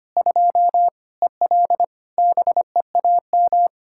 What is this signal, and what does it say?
Morse code, heard as a keyed tone, about 25 wpm, 700 Hz. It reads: 2 EL BEAM
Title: Practice QSO - Exchange 2 - Info - 25wpm